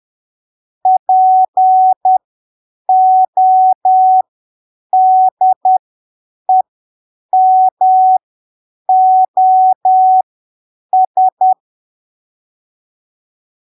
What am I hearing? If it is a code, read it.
Morse code, heard as a keyed tone, about 10 wpm, spaced as for 7 wpm, 750 Hz. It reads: PODEMOS
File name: arquivos / texto / texto 04 10ppm